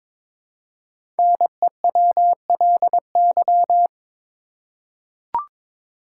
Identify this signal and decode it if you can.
Morse code, heard as a keyed tone, about 22 wpm, 700 Hz. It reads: NEWLY
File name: 5 Letter Words - First 3 Letters Given - 22wpm